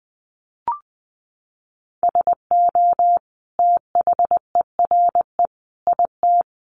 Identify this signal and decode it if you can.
Morse code, heard as a keyed tone, about 20 wpm, 700 Hz. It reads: SO THERE IT
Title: Sentences from Top 100 Words 20wpm